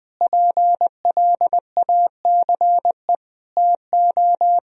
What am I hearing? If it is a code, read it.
Morse code, heard as a keyed tone, about 20 wpm, 700 Hz. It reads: PLACE TO
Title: Sentences from Top 1000 Words - Review Part 1 - 20wpm